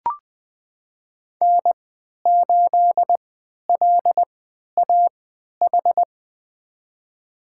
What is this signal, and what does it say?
Morse code, heard as a keyed tone, about 20 wpm, spaced as for 11 wpm, 700 Hz. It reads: N8LAH
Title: USA Call Signs - 5 Characters - 3x Character Spacing - 20wpm